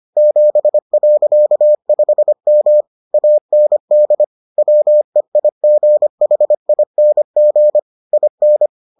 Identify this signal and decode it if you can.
Morse code, heard as a keyed tone, about 25 wpm, 600 Hz. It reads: 7.5M AND WEIGHING IN